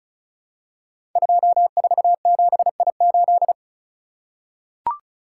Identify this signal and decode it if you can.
Morse code, heard as a keyed tone, about 36 wpm, 690 Hz. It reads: 247I8
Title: Code Groups - 5 Letter:Numbers - 35wpm